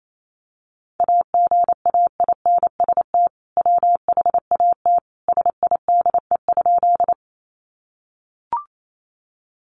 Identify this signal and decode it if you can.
Morse code, heard as a keyed tone, about 28 wpm, 700 Hz. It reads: AGAINST WHAT SIDE?
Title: Sentences from Top 400 Words 28wpm